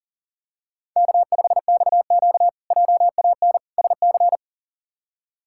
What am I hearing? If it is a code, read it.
Morse code, heard as a keyed tone, about 40 wpm, 700 Hz. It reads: K5XQ JAN SC
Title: K1USN SST QSO Practice - 40wpm